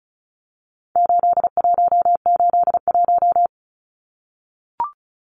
Morse code, heard as a keyed tone, about 34 wpm, 700 Hz. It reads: 8181
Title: Numbers - 4-Digit 35wpm